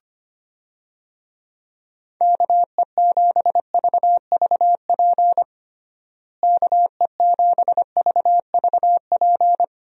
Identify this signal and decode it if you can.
Morse code, heard as a keyed tone, about 25 wpm, 700 Hz. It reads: KE7VVP KE7VVP
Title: USA Call Signs - 6 Characters - Repeated 3x - 25wpm